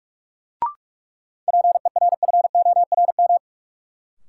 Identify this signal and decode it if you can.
Morse code, heard as a keyed tone, about 44 wpm, 700 Hz. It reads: PERFORM
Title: Sentences from Top 1500 Words - New Word Drill Alt - 45wpm